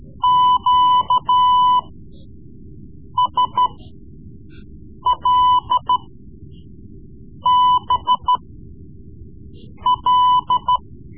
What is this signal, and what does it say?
Morse code, heard as a keyed tone, about 13 wpm, spaced as for 5 wpm, 1000 Hz. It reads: QSLBL
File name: Morse Lessons / Morse Lessons / Tape-3B